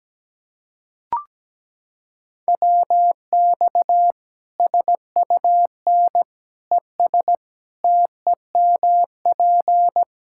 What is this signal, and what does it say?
Morse code, heard as a keyed tone, about 17 wpm, 700 Hz. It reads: WX SUN ES TEMP
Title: K1USN SST QSO Practice 17wpm